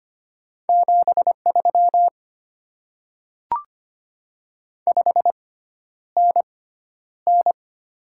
Morse code, heard as a keyed tone, about 25 wpm, spaced as for 11 wpm, 690 Hz. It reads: 73 5NN
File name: Top 20 QSO Elements - 6x Character Spacing - 25wpm